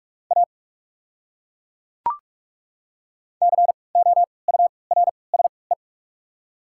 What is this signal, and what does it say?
Morse code, heard as a keyed tone, about 43 wpm, 700 Hz. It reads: A C O U R S E
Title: 3x Character Spacing - Top 500 Words - 45wpm